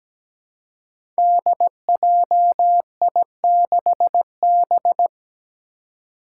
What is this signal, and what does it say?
Morse code, heard as a keyed tone, about 17 wpm, 700 Hz. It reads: DJI6B
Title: Five Challenging Letter-Number Combinations 17wpm